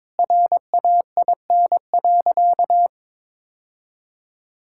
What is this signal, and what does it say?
Morse code, heard as a keyed tone, about 22 wpm, 700 Hz. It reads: RAIN.
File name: Difficult Sentences with Hints - 22wpm